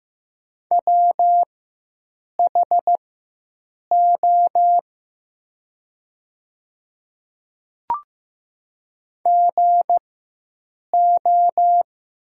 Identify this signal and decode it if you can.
Morse code, heard as a keyed tone, about 15 wpm, spaced as for 7 wpm, 700 Hz. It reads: WHO GO
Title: 4x Character Spacing - Top 100 Words - 15wpm